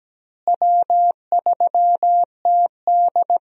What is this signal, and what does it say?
Morse code, heard as a keyed tone, about 17 wpm, 700 Hz. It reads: W3TD
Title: USA Call Signs - 4 Characters - Repeated 3x - 17wpm